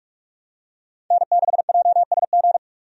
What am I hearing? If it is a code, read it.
Morse code, heard as a keyed tone, about 46 wpm, 700 Hz. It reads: N6JSG
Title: USA Call Signs - 5 Characters - 45wpm